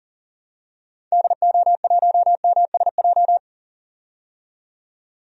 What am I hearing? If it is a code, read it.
Morse code, heard as a keyed tone, about 40 wpm, 700 Hz. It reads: DO1MSJ
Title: DXCC-100 Call Signs - 6 Characters - 40wpm